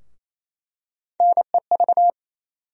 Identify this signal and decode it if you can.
Morse code, heard as a keyed tone, about 28 wpm, 700 Hz. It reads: NEV
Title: USA Call Sign Postfixes - 3 Letters 28wpm